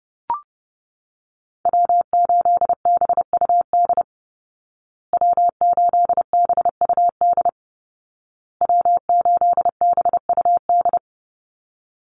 Morse code, heard as a keyed tone, about 30 wpm, 700 Hz. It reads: W8BUD W8BUD W8BUD
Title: Call Signs Repeated 2x - 30wpm